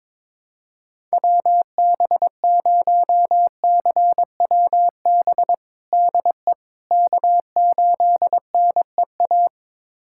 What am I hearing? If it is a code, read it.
Morse code, heard as a keyed tone, about 22 wpm, 700 Hz. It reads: WB0CWB DE K8NEA
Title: USA Callsign DE Callsign 22wpm